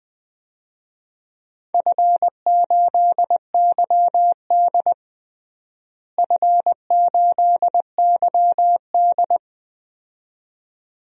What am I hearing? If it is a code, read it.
Morse code, heard as a keyed tone, about 20 wpm, 700 Hz. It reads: F8YD F8YD
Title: DXCC-100 Call Signs - 4 Characters - Repeated 2x - 20wpm